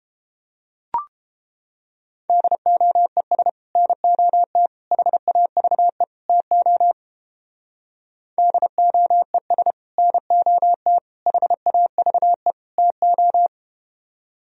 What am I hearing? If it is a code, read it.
Morse code, heard as a keyed tone, about 32 wpm, 700 Hz. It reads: DOES NOT HAVE TO DOES NOT HAVE TO